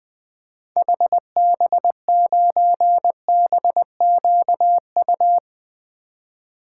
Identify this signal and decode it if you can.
Morse code, heard as a keyed tone, about 20 wpm, 700 Hz. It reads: HB9BQU
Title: 2023 SOTA Activator Call Signs - 20wpm